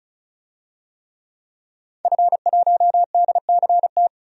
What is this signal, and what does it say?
Morse code, heard as a keyed tone, about 35 wpm, 700 Hz. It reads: F1DCT
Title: DXCC-100 Call Signs - 5 Characters - 35wpm